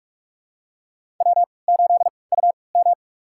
Morse code, heard as a keyed tone, about 46 wpm, spaced as for 26 wpm, 700 Hz. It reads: W8UM
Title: USA Call Signs - 3x Character Spacing - 45wpm